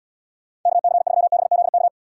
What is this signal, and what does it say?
Morse code, heard as a keyed tone, about 79 wpm, 700 Hz. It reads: KC3BZK